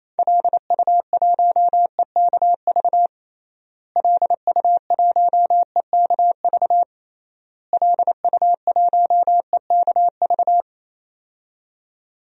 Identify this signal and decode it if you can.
Morse code, heard as a keyed tone, about 28 wpm, 700 Hz. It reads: LU1EKV LU1EKV LU1EKV
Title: DXCC-100 Call Signs - 6 Characters - Repeated 3x - 28wpm